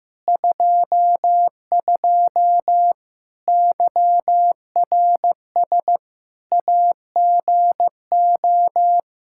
Morse code, heard as a keyed tone, about 15 wpm, 700 Hz. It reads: 22 YRS AGO
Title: Practice QSO - Exchange 3 - More Info - 15wpm